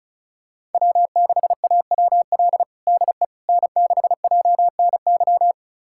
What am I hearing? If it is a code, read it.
Morse code, heard as a keyed tone, about 34 wpm, 700 Hz. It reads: W6AWL DE N6JNY
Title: USA Callsign DE Callsign - 5 Characters - 35wpm